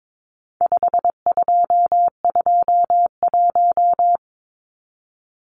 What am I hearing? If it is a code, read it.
Morse code, heard as a keyed tone, about 22 wpm, 700 Hz. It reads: 5221
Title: Numbers - 4-Digit 22wpm